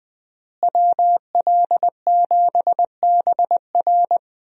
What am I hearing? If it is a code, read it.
Morse code, heard as a keyed tone, about 21 wpm, 700 Hz. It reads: WL7BR
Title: USA Call Signs - 5 Characters - Repeated 3x - 20wpm